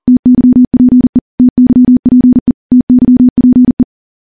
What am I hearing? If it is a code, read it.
Morse code, heard as a keyed tone, about 43 wpm, 250 Hz. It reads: TYPE TYPE TYPE
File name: morse / morse-training-40WPM